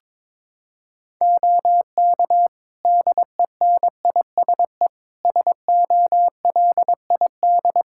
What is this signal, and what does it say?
Morse code, heard as a keyed tone, about 21 wpm, 700 Hz. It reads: OK DENISE SOLID
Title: Practice QSO - Exchange 3 - More Info - 22wpm